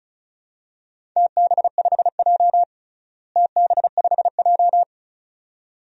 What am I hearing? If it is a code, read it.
Morse code, heard as a keyed tone, about 34 wpm, 700 Hz. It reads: TB5J TB5J
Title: DXCC-100 Call Signs - 4 Characters - Repeated 2x - 35wpm